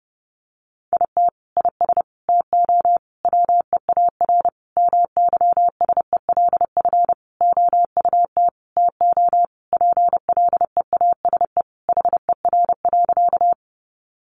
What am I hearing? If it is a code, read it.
Morse code, heard as a keyed tone, about 30 wpm, 700 Hz. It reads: IT IS TO WEAR MYSELF OUT TO PLEASE HER.
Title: Sentences from Top 1500 Words 30wpm